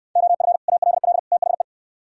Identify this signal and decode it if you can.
Morse code, heard as a keyed tone, about 63 wpm, 700 Hz. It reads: OF ALL THE